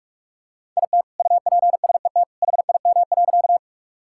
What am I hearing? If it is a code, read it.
Morse code, heard as a keyed tone, about 46 wpm, 700 Hz. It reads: IT UPSET HIM.
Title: Sentences from Top 1000 Words - Review Part 5 - 45wpm